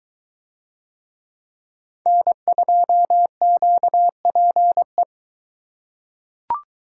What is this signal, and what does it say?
Morse code, heard as a keyed tone, about 22 wpm, 700 Hz. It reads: N2QPE